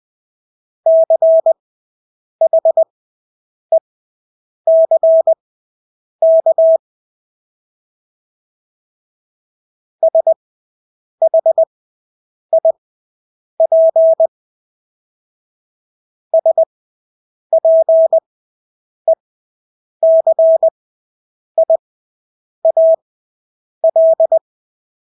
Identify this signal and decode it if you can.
Morse code, heard as a keyed tone, about 20 wpm, spaced as for 8 wpm, 650 Hz. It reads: CHECK SHIP SPECIAL
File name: Engelske ord 20/8 wpm